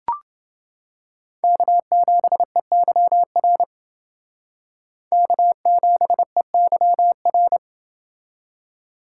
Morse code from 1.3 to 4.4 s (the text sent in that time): K7EYR